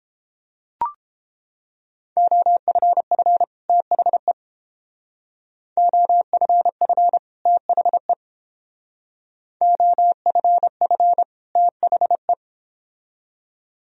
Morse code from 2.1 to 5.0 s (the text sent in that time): OFF THE